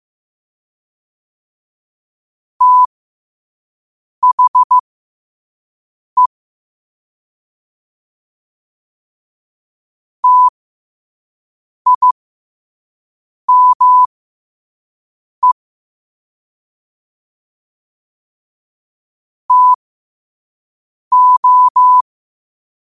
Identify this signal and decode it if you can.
Morse code, heard as a keyed tone, about 15 wpm, spaced as for 5 wpm, 1000 Hz. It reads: THE TIME TO
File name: qotdfiles / 5wpm